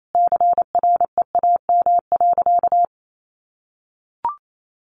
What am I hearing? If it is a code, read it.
Morse code, heard as a keyed tone, about 28 wpm, 700 Hz. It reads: CREAM.
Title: Sentences from Top 1500 Words 28wpm